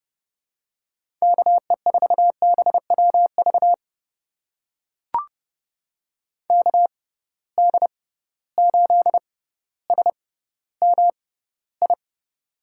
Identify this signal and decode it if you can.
Morse code, heard as a keyed tone, about 30 wpm, spaced as for 15 wpm, 700 Hz. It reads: KE4BWV KD8SMI
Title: USA Call Signs - 6 Characters - 6x Character Spacing - 30wpm